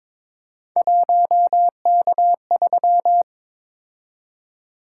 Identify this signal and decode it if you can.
Morse code, heard as a keyed tone, about 22 wpm, 700 Hz. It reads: 1K3
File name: practice / 22wpm / Three Letter-Number Combinations - 22wpm